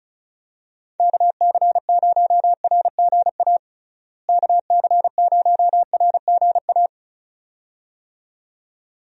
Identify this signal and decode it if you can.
Morse code, heard as a keyed tone, about 34 wpm, 700 Hz. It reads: KC0RGA KC0RGA
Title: USA Call Signs - 6 Characters - Repeated 2x - 35wpm